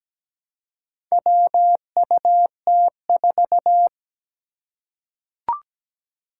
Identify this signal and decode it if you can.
Morse code, heard as a keyed tone, about 17 wpm, 700 Hz. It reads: WUT4
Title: Four Challenging Letter-Number Combinations 17wpm